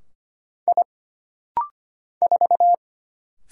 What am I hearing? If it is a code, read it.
Morse code, heard as a keyed tone, about 25 wpm, 700 Hz. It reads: I 4
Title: Single Letter-Number - Rapid-Fire 25wpm